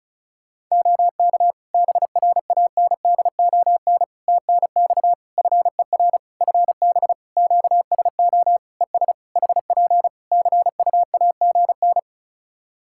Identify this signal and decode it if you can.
Morse code, heard as a keyed tone, about 36 wpm, 700 Hz. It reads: OK BRANDON TNX FER FB QSO ES HP CUAGN
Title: K1USN SST QSO Practice 35wpm